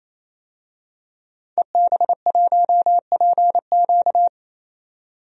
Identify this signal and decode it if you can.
Morse code, heard as a keyed tone, about 28 wpm, 700 Hz. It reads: EB1PQ
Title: DXCC-100 Call Signs - 5 Characters - 28wpm